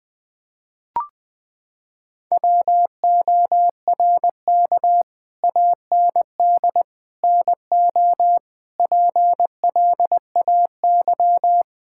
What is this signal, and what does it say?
Morse code, heard as a keyed tone, about 20 wpm, 700 Hz. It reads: WORK AND NO PLAY
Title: Sets of 8 Words - 20wpm